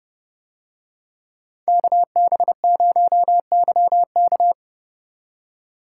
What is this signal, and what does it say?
Morse code, heard as a keyed tone, about 29 wpm, 700 Hz. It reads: KB0YK